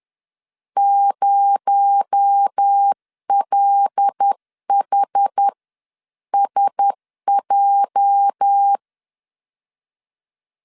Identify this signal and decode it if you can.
Morse code, heard as a keyed tone, about 10 wpm, spaced as for 9 wpm, 800 Hz. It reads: 0LHSJ
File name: LEZIONI / 9 28 10wpm 2842